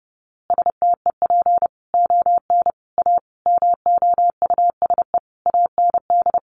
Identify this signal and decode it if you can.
Morse code, heard as a keyed tone, about 30 wpm, 700 Hz. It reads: STEP ON A MOUSE AND
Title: A Sound Of Thunder 30wpm